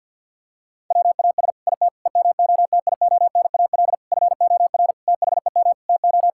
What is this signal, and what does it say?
Morse code, heard as a keyed tone, about 52 wpm, 700 Hz. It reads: WAS IT EMOTIONAL FOR THEM TO
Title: Sentences from Top 1000 Words - Review Part 5 - 50wpm